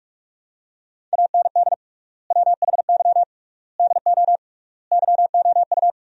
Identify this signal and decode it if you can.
Morse code, heard as a keyed tone, about 46 wpm, 700 Hz. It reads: AND WHY DO YOU
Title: 3x Word-Spacing - Sentences from Top 400 Words - 45wpm